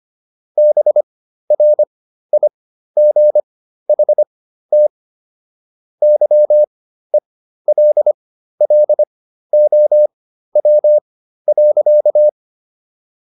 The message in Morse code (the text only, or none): BRIGHT YELLOW.